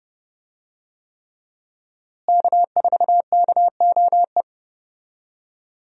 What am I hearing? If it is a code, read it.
Morse code, heard as a keyed tone, about 32 wpm, 700 Hz. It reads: K4KOE
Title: USA Call Signs - 5 Characters - Repeated 3x - 30wpm